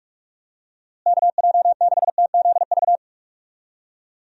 Morse code, heard as a keyed tone, about 46 wpm, 700 Hz. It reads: KJ6TZV
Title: USA Call Signs - 6 Characters - Repeated 2x - 45wpm